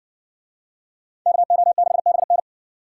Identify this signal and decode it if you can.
Morse code, heard as a keyed tone, about 60 wpm, 700 Hz. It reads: KO6BN